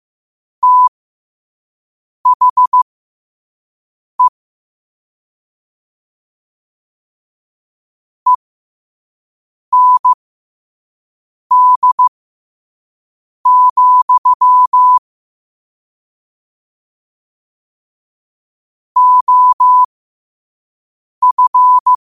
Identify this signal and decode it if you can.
Morse code, heard as a keyed tone, about 15 wpm, spaced as for 5 wpm, 1000 Hz. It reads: THE END, OF
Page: New quotes every day in morse code at 5 Words per minute.